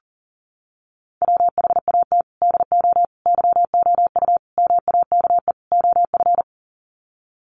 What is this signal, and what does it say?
Morse code, heard as a keyed tone, about 40 wpm, 700 Hz. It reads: WHAT DO YOU MAKE OF
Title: Sets of 5 Words 40wpm